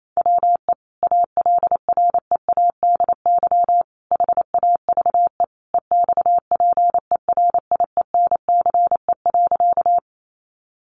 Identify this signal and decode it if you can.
Morse code, heard as a keyed tone, about 28 wpm, 700 Hz. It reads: WE ALREADY HAVE EXPERIENCE.
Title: Sentences from Top 700 Words 28wpm